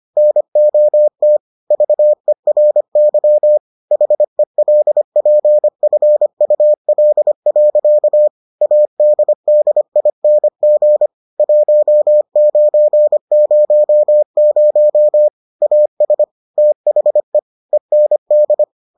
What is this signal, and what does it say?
Morse code, heard as a keyed tone, about 25 wpm, 600 Hz. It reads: NOT VERY HELPFUL. ADDING 1900 AS THE END